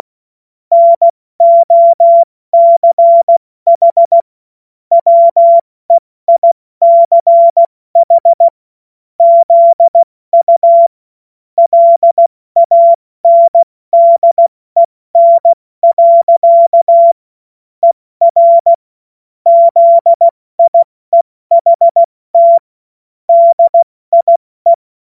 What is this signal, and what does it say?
Morse code, heard as a keyed tone, about 16 wpm, 680 Hz. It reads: NOCH WEICH ZU LANDEN. ER ZIEHT DIE
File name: Morsekurs / Witze / Witz 07. Dezember 2021, 70 F80